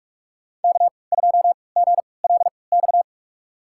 Morse code, heard as a keyed tone, about 43 wpm, spaced as for 25 wpm, 700 Hz. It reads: K2GLX
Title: USA Call Signs - 5 Characters - 3x Character Spacing - 45wpm